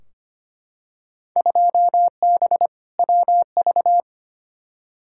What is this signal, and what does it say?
Morse code, heard as a keyed tone, about 25 wpm, 700 Hz. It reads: 2B WV
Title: Field Day Exchanges Encore - 25wpm